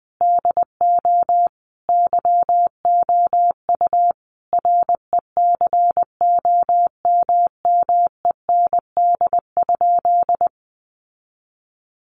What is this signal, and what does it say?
Morse code, heard as a keyed tone, about 19 wpm, 700 Hz. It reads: DO YOU RECOMMEND?
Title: Sentences from Top 1200 Words 20wpm